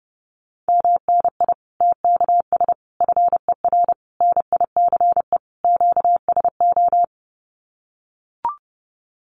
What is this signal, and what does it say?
Morse code, heard as a keyed tone, about 30 wpm, 700 Hz. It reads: MNI TKS FER NICE QSO
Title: Common QSO Phrases 30wpm